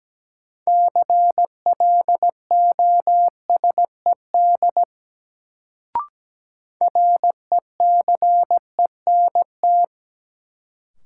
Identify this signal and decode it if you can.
Morse code, heard as a keyed tone, about 17 wpm, 700 Hz. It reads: CLOSED RECENT
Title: Sentences from Top 1400 Words - New Word Drill Alt - 17wpm